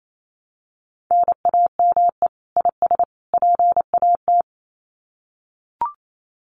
Morse code, heard as a keyed tone, about 28 wpm, 700 Hz. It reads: NAME IS PAT
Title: Common QSO Phrases 28wpm